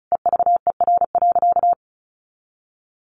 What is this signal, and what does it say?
Morse code, heard as a keyed tone, about 34 wpm, 700 Hz. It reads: EVER.
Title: Sentences from Top 1400 Words 35wpm